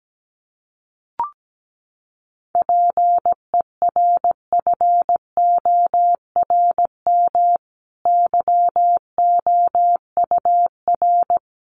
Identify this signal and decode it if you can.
Morse code, heard as a keyed tone, about 17 wpm, 700 Hz. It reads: PERFORM YOUR
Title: Sentences from Top 1500 Words 17wpm